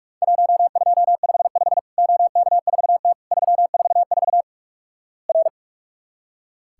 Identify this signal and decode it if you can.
Morse code, heard as a keyed tone, about 46 wpm, 700 Hz. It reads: 1255 OK4T 344 R
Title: Worked All Europe DX Contest - QTC Exchange - 45wpm